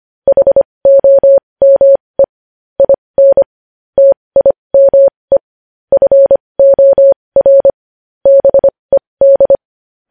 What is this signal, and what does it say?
Morse code, heard as a keyed tone, about 25 wpm, spaced as for 20 wpm, 550 Hz. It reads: HOME IN TIME FOR BED
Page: news-headlines-in-morse-code-at-20-wpm